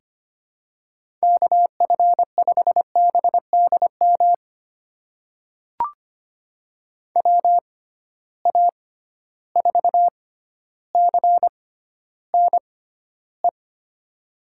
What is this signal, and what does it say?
Morse code, heard as a keyed tone, about 25 wpm, spaced as for 13 wpm, 700 Hz. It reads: KF5BDM WA4CNE
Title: USA Call Signs - 6 Characters - 6x Character Spacing - 25wpm